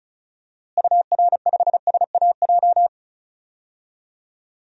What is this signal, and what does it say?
Morse code, heard as a keyed tone, about 34 wpm, 700 Hz. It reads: UR5SAJ
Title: DXCC-100 Call Signs - 6 Characters - Repeated 3x - 35wpm